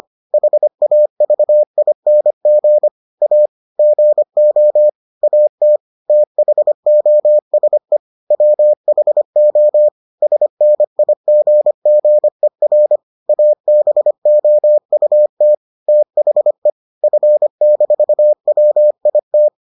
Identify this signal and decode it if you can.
Morse code, heard as a keyed tone, about 25 wpm, 600 Hz. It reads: HAVING A GO AT THOSE WHO SNIGGER ABOUT THE F-WIT